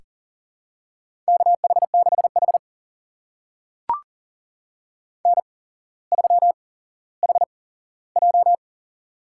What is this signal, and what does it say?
Morse code, heard as a keyed tone, about 40 wpm, spaced as for 17 wpm, 700 Hz. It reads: KH6H N3HJ
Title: USA Call Signs - 4 Characters - 8x Character Spacing - 40wpm